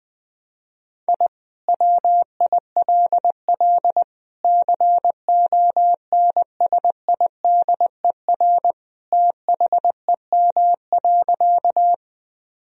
Morse code, heard as a keyed tone, about 20 wpm, 700 Hz. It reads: I WILL CONSIDER THEM.
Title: Sentences from Top 1000 Words - Review Part 4 - 20wpm